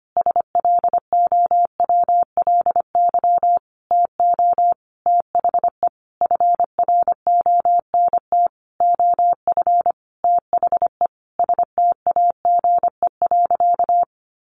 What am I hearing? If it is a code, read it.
Morse code, heard as a keyed tone, about 25 wpm, 700 Hz. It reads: SLOWLY TO THE FRONT OF THE STAGE.
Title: Sentences from Top 1400 Words 25wpm